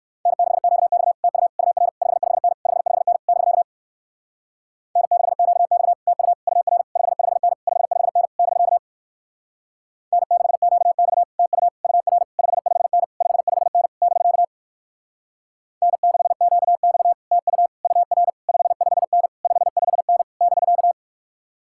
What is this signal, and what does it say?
Morse code, heard as a keyed tone, about 55 wpm, 700 Hz. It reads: N6QX TU UR 55N 55N <BK> N6QX TU UR 55N 55N <BK> N6QX TU UR 55N 55N <BK> N6QX TU UR 55N 55N <BK>
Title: Speed-Racing Common QSO Phrases Encore - 45wpm